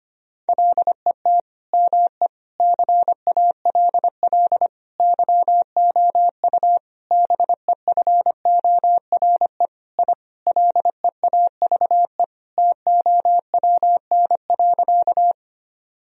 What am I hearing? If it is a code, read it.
Morse code, heard as a keyed tone, about 25 wpm, 700 Hz. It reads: LET ME CALL YOU BEFORE I LEAVE TOWN.
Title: Sentences from Top 1000 Words - Review Part 2 - 25wpm